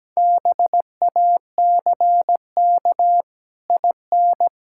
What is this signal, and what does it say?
Morse code, heard as a keyed tone, about 17 wpm, 700 Hz. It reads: BACK IN
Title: Pangram Challenge v2 - 17wpm